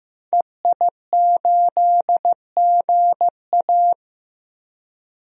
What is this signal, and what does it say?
Morse code, heard as a keyed tone, about 16 wpm, 700 Hz. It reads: EI8GA
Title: DXCC-100 Call Signs - 5 Characters - Repeated 3x - 15wpm